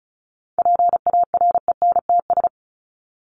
Morse code, heard as a keyed tone, about 34 wpm, 700 Hz. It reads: PARENTS
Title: ES:S Suffix - Top 100 Words with Base Words 35wpm